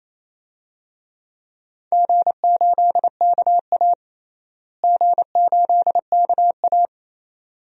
Morse code, heard as a keyed tone, about 27 wpm, 700 Hz. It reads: G8KA G8KA
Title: DXCC-100 Call Signs - 4 Characters - Repeated 2x - 28wpm